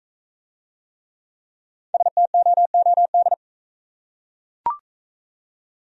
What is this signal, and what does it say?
Morse code, heard as a keyed tone, about 40 wpm, 700 Hz. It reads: STOOD